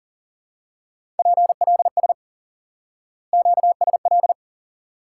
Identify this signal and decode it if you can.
Morse code, heard as a keyed tone, about 38 wpm, 700 Hz. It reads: PLS QSL
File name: IQE Course 3 - Lesson 89 - YAGI - Two Elements - 40wpm